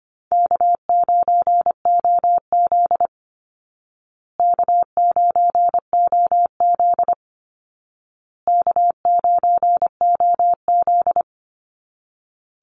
Morse code, heard as a keyed tone, about 25 wpm, 700 Hz. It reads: K9OZ K9OZ K9OZ
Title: Easy Call Signs Repeated 3x - 25wpm